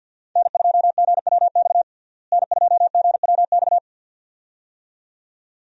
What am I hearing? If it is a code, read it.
Morse code, heard as a keyed tone, about 49 wpm, 700 Hz. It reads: N2GWX N2GWX
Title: USA Call Signs - 5 Characters - Repeated 2x - 50wpm